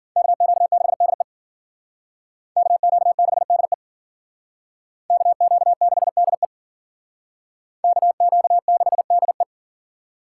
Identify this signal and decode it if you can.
Morse code, heard as a keyed tone, about 50 wpm, 700 Hz. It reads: KQ6DE KQ6DE KQ6DE KQ6DE